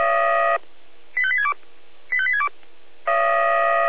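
Five-tone selective call (ZVEI), heard as quick runs of five tones, tones of about 70 ms. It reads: 85862 85862
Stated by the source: ZVEI five-tone call received on an SDR, Feuerwehr Orlamünde fire brigade.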